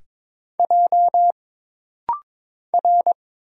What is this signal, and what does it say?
Morse code, heard as a keyed tone, about 22 wpm, 700 Hz. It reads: J R
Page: This rapid-fire timing is geared to push your ICR (Instant Character Recognition) skills.